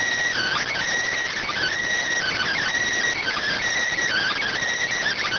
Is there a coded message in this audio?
Yes: Emergency Alert System (SAME) tones (header bursts) from 0.3 to 4.5 s, heard as harsh buzzing bursts.